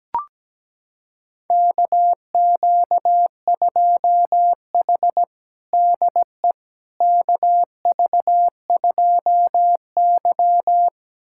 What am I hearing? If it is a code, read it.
Morse code, heard as a keyed tone, about 17 wpm, 700 Hz. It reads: KQ2H DE KV2Y
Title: USA Callsign DE Callsign - 4 Characters - 17wpm